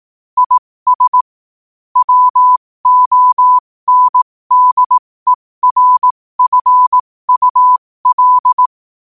Morse code, heard as a keyed tone, about 18 wpm, spaced as for 15 wpm, 1000 Hz. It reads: IS WONDERFUL
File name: qotdfiles / 15wpm